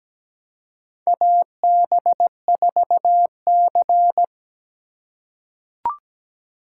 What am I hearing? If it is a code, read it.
Morse code, heard as a keyed tone, about 17 wpm, 700 Hz. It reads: AB4C
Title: USA Call Signs - 4 Characters - 2x Character Spacing - 17wpm